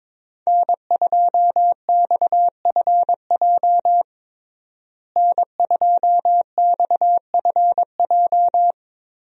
Morse code, heard as a keyed tone, about 22 wpm, 700 Hz. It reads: N2XFJ N2XFJ
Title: USA Call Signs - 5 Characters - Repeated 3x - 22wpm